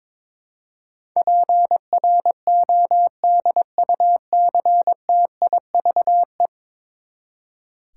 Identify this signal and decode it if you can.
Morse code, heard as a keyed tone, about 22 wpm, 700 Hz. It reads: PRODUCTIVE
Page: morse-code-ninja-practice-22wpm